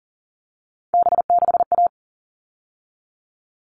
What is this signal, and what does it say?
Morse code, heard as a keyed tone, about 40 wpm, 700 Hz. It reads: B6A
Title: All Cut Numbers - 3-Digit 40wpm